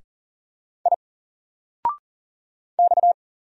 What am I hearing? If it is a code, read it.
Morse code, heard as a keyed tone, about 40 wpm, 700 Hz. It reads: I X
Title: Single Letter-Number 40wpm